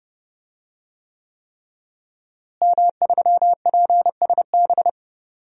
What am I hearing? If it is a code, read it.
Morse code, heard as a keyed tone, about 30 wpm, 700 Hz. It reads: M3PSB
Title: DXCC-100 Call Signs - 5 Characters - Repeated 3x - 30wpm